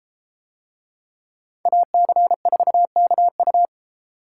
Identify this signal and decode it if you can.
Morse code, heard as a keyed tone, about 32 wpm, 700 Hz. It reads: AC4KU